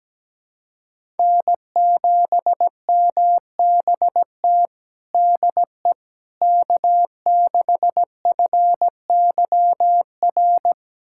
Morse code, heard as a keyed tone, about 17 wpm, 700 Hz. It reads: N7MBT DE K6FYR
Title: USA Callsign DE Callsign - 5 Characters - 17wpm